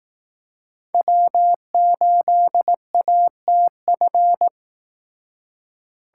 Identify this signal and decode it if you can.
Morse code, heard as a keyed tone, about 18 wpm, 700 Hz. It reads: W8ATF
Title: Speed Racing - USA Call Signs 15wpm